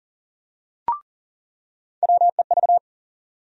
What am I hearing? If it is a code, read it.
Morse code, heard as a keyed tone, about 40 wpm, 700 Hz. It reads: WEV
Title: USA Call Sign Postfixes - 3 Letters 40wpm